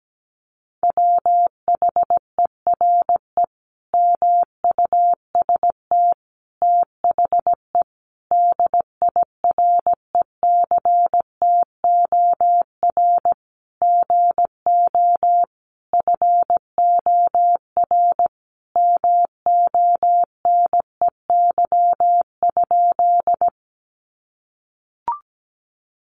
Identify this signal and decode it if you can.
Morse code, heard as a keyed tone, about 17 wpm, 700 Hz. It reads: WHERE MUST THE DIRECTOR GO FOR MONEY?
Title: Sentences from Top 1300 Words 17wpm